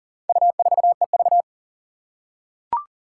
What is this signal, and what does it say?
Morse code, heard as a keyed tone, about 40 wpm, 700 Hz. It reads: U4EV
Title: Four Challenging Letter-Number Combinations 40wpm